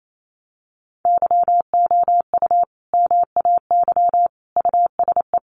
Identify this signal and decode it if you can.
Morse code, heard as a keyed tone, about 28 wpm, 700 Hz. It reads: YOU MAY USE
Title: Sentences from Top 1400 Words 28wpm